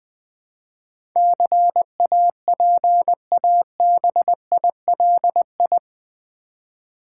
Morse code, heard as a keyed tone, about 20 wpm, 700 Hz. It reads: CAPABILI
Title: Build A Word - 4 Letters At A Time - 10-Letter Words - 20wpm